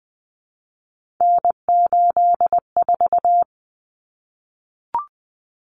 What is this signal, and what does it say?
Morse code, heard as a keyed tone, about 20 wpm, 700 Hz. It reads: N84
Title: Common Cut Numbers - 3-Digit 20wpm